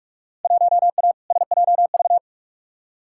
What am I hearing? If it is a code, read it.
Morse code, heard as a keyed tone, about 46 wpm, 700 Hz. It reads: 1A SJV
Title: Field Day Exchanges Encore - 45wpm